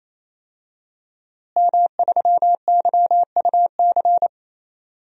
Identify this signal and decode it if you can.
Morse code, heard as a keyed tone, about 28 wpm, 700 Hz. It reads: M3YUC
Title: DXCC-100 Call Signs - 5 Characters - Repeated 3x - 28wpm